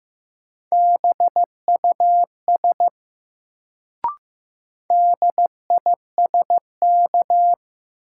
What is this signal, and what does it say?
Morse code, heard as a keyed tone, about 15 wpm, 700 Hz. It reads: BUS DISK
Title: Sentences from Top 1000 Words - Review Part 6 - New Word Drill - 15wpm